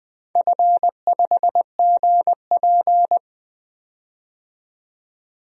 Morse code, heard as a keyed tone, about 20 wpm, 700 Hz. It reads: F5GP
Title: DXCC-100 Call Signs - 4 Characters - 20wpm